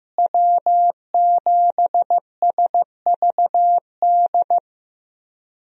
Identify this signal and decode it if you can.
Morse code, heard as a keyed tone, about 16 wpm, 700 Hz. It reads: W7SVD
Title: USA Call Signs - 5 Characters - Repeated 2x - 15wpm